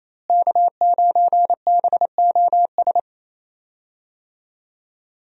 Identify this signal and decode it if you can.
Morse code, heard as a keyed tone, about 28 wpm, 700 Hz. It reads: K9BOS
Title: USA Call Signs - 5 Characters - Repeated 3x - 28wpm